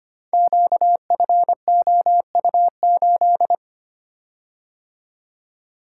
Morse code, heard as a keyed tone, about 25 wpm, 700 Hz. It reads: QFOU8
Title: Code Groups - 5 Letter:Numbers - 25wpm